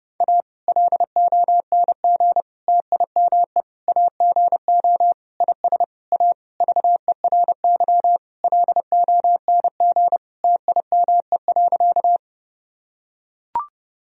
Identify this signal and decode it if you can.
Morse code, heard as a keyed tone, about 30 wpm, 700 Hz. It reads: A LONG TIME AGO IS A VERY LONG TIME.
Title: Sentences from Top 1000 Words - Review Part 3 - 30wpm